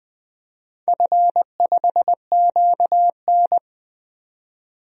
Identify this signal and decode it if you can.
Morse code, heard as a keyed tone, about 20 wpm, 700 Hz. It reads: F5QN
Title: DXCC-100 Call Signs - 4 Characters - 20wpm